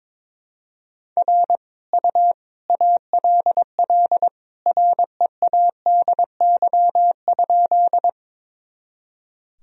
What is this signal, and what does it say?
Morse code, heard as a keyed tone, about 22 wpm, 700 Hz. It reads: R U ALL READY?
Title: Sentences from Top 400 Words and Abbreviations 22wpm